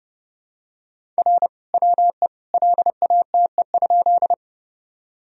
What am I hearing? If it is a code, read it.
Morse code, heard as a keyed tone, about 29 wpm, 700 Hz. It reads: R WE LATE?
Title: Sentences from Top 300 Words and Abbreviations 30wpm